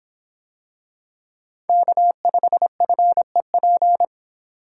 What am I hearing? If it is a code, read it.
Morse code, heard as a keyed tone, about 24 wpm, 700 Hz. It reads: K5FEP